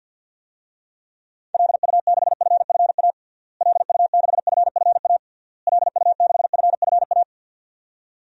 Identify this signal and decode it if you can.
Morse code, heard as a keyed tone, about 51 wpm, 700 Hz. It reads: LU6FFA LU6FFA LU6FFA
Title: DXCC-100 Call Signs - 6 Characters - Repeated 3x - 50wpm